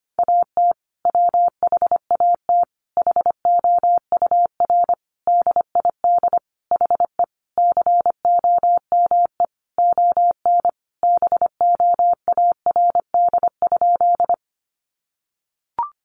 Morse code, heard as a keyed tone, about 25 wpm, 700 Hz. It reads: AT WHAT HOUR DID HE COME ON BOARD?
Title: Sentences from Top 700 Words 25wpm